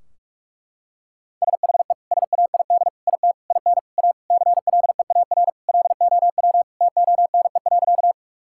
Morse code, heard as a keyed tone, about 41 wpm, 700 Hz. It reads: SHE SAID IT IN A CLEAR LOW TONE.
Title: Sentences from Top 1600 Words 45wpm